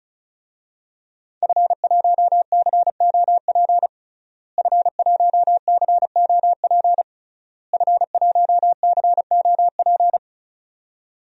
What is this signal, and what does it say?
Morse code, heard as a keyed tone, about 36 wpm, 700 Hz. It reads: F1COP F1COP F1COP
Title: DXCC-100 Call Signs - 5 Characters - Repeated 3x - 35wpm